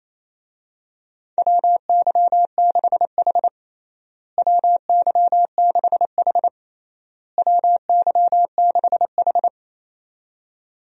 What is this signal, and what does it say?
Morse code, heard as a keyed tone, about 28 wpm, 700 Hz. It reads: WY6H WY6H WY6H
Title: USA Call Signs - 4 Characters - Repeated 3x - 28wpm